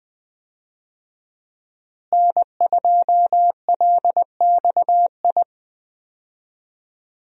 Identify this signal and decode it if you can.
Morse code, heard as a keyed tone, about 21 wpm, 700 Hz. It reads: N2LXI
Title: USA Call Signs - 5 Characters - Repeated 3x - 20wpm